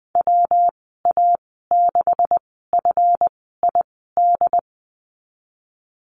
Morse code, heard as a keyed tone, about 20 wpm, spaced as for 15 wpm, 700 Hz. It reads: WA6FID
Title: 173 Lesson 28g - Number 9 - Call Signs 20wpm